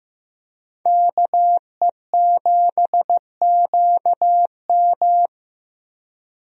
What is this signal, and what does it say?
Morse code, heard as a keyed tone, about 16 wpm, 700 Hz. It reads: KE7QM
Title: USA Call Signs - 5 Characters - Repeated 2x - 15wpm